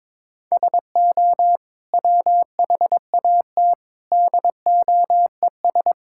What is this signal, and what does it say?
Morse code, heard as a keyed tone, about 22 wpm, 700 Hz. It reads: SO WHAT DOES
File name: practice / 22wpm / Sentences with 8-Word Phrases - 22wpm